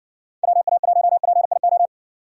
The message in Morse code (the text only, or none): WA1PIO